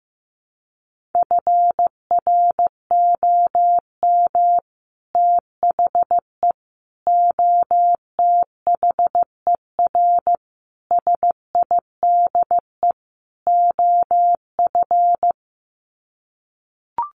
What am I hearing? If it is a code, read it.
Morse code, heard as a keyed tone, about 15 wpm, 700 Hz. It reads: FROM THE OTHER SIDE OF
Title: Sets of 5 Words Encore 15wpm